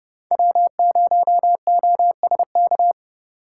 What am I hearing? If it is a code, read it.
Morse code, heard as a keyed tone, about 30 wpm, 700 Hz. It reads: W0OSK
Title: USA Call Signs - 5 Characters - 3x Character Spacing - 30wpm